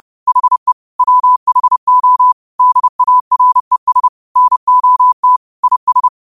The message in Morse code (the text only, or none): HE WHO DARES NOT IS